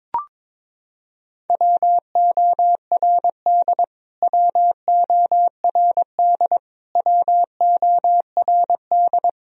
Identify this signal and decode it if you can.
Morse code, heard as a keyed tone, about 22 wpm, 700 Hz. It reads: WORD WORD WORD
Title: Top 100 Words Repeated 22wpm